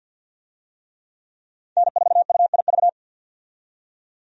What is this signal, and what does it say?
Morse code, heard as a keyed tone, about 53 wpm, 700 Hz. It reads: N4UIV
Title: USA Call Signs - 5 Characters - 50wpm